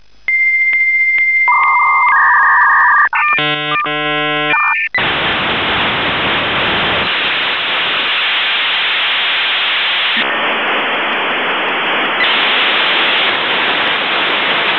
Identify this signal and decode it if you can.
Dial-up modem handshake.
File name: modem / k56flexrock1